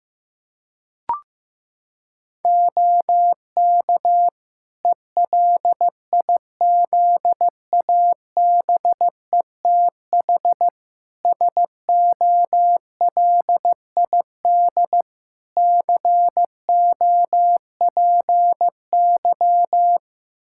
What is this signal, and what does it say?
Morse code, heard as a keyed tone, about 15 wpm, 700 Hz. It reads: OK ELIZABETH SOLID COPY
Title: Practice QSO - Exchange 3 - More Info - 15wpm